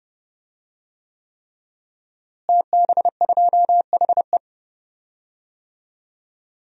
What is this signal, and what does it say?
Morse code, heard as a keyed tone, about 29 wpm, 700 Hz. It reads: TB2HE
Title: DXCC-100 Call Signs - 5 Characters - Repeated 3x - 30wpm